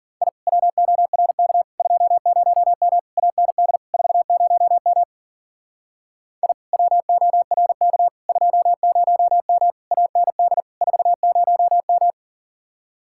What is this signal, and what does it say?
Morse code, heard as a keyed tone, about 50 wpm, 700 Hz. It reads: I WORK 20M AND 40M I WORK 20M AND 40M
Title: Speed-Racing Common QSO Phrases - 40wpm